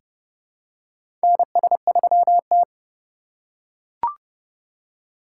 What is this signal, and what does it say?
Morse code, heard as a keyed tone, about 30 wpm, 700 Hz. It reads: NS3T
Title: USA Call Signs - 2x Character Spacing - 30wpm